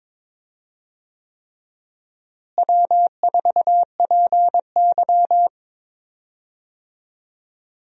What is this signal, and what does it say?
Morse code, heard as a keyed tone, about 22 wpm, 700 Hz. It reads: W4PY
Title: USA Call Signs - 4 Characters - Repeated 3x - 22wpm